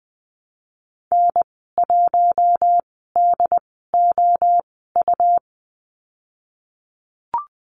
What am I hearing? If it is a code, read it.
Morse code, heard as a keyed tone, about 20 wpm, spaced as for 14 wpm, 700 Hz. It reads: N1DOU
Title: 161 Lesson 26g - Letter g - Call Signs 20wpm